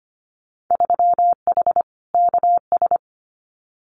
Morse code, heard as a keyed tone, about 25 wpm, 700 Hz. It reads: 3H KS
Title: Winter Field Day Exchanges Encore 25wpm